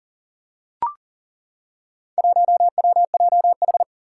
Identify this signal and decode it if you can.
Morse code, heard as a keyed tone, about 40 wpm, 700 Hz. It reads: 1WJH
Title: Four Challenging Letter-Number Combinations 40wpm